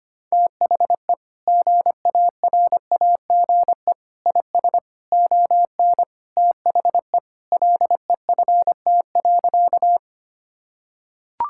Morse code, heard as a keyed tone, about 25 wpm, 700 Hz. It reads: THE GARAGE IS ON THE LEFT.
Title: Sentences from Top 1600 Words 25wpm